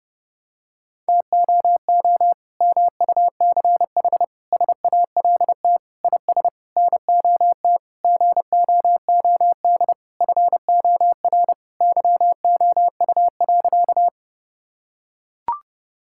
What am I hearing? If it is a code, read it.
Morse code, heard as a keyed tone, about 32 wpm, 700 Hz. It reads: TOO MUCH SALT IS NOT GOOD FOR YOU.
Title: Sentences from Top 1000 Words - Review Part 5 - 30wpm